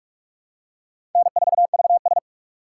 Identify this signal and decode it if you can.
Morse code, heard as a keyed tone, about 46 wpm, 700 Hz. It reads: N4VS
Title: USA Call Signs - 4 Characters - 6x Character Spacing - 45wpm